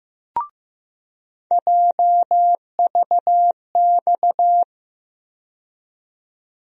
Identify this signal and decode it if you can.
Morse code, heard as a keyed tone, about 15 wpm, 700 Hz. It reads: JVX
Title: USA Call Sign Postfixes - 3 Letters 15wpm